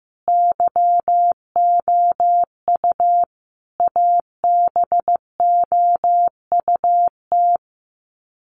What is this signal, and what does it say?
Morse code, heard as a keyed tone, about 15 wpm, 700 Hz. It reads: YOU ABOUT
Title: Sets of 5 Words 15wpm